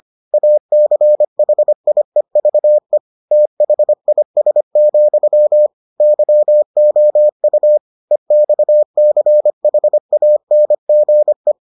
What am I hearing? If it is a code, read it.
Morse code, heard as a keyed tone, about 25 wpm, 600 Hz. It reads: ACHIEVE THIS, YOU EXCHANGE